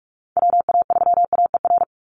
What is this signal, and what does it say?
Morse code, heard as a keyed tone, about 44 wpm, 700 Hz. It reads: WA3AER
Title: Call Signs Repeated 3x - 45wpm